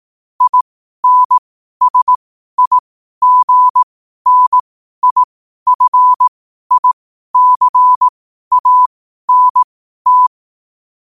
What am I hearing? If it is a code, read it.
Morse code, heard as a keyed tone, about 18 wpm, spaced as for 12 wpm, 1000 Hz. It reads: INSIGNIFICANT
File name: qotdfiles / 12wpm